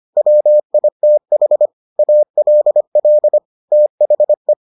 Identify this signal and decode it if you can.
Morse code, heard as a keyed tone, about 25 wpm, 600 Hz. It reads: WITH ALL THE